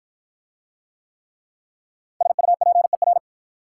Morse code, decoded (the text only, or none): SUPER